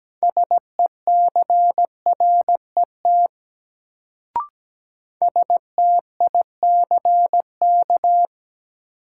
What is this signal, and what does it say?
Morse code, heard as a keyed tone, about 17 wpm, 700 Hz. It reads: SECRET STICK
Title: Sentences from Top 1200 Words - New Word Drill - 17wpm